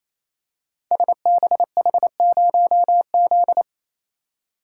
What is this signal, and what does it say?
Morse code, heard as a keyed tone, about 27 wpm, 700 Hz. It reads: SBH0Z
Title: Code Groups - 5 Letter:Numbers - 28wpm